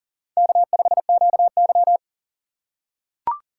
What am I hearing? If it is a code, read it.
Morse code, heard as a keyed tone, about 37 wpm, 700 Hz. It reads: K5QY
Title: USA Call Signs - 4 Characters - 8x Character Spacing - 40wpm